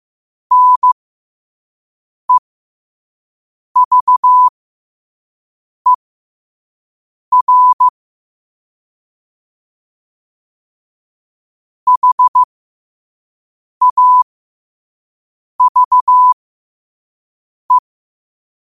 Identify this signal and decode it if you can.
Morse code, heard as a keyed tone, about 15 wpm, spaced as for 5 wpm, 1000 Hz. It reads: NEVER HAVE
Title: 5 WPM morse code quotes for Fri, 01 Mar 2024